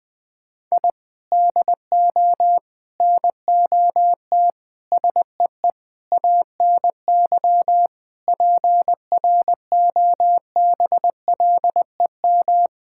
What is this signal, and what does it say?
Morse code, heard as a keyed tone, about 20 wpm, 700 Hz. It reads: I DO NOT SEE ANY PROBLEM
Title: Sentences with 5-Word Phrases - 20wpm